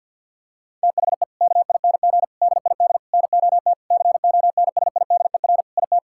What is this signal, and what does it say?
Morse code, heard as a keyed tone, about 50 wpm, 700 Hz. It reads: THE KING DID NOT CONSIDER IT
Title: Sentences from Top 1000 Words - Review Part 5 - 50wpm